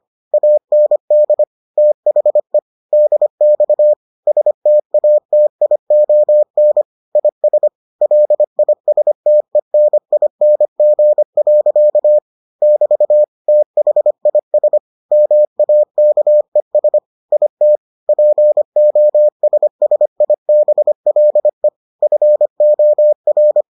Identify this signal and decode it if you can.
Morse code, heard as a keyed tone, about 25 wpm, 600 Hz. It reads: AND THE DX STATION IS LISTENING. <BT> THIS MAKES IT POSSIBLE FOR